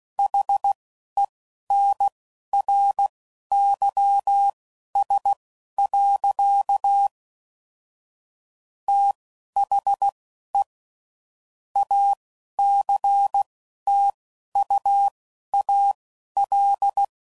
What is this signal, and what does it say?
Morse code, heard as a keyed tone, about 16 wpm, spaced as for 11 wpm, 800 Hz. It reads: HENRYS. THE ACTUAL
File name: CW Code Groups @10WPM / Technical Subjects / XL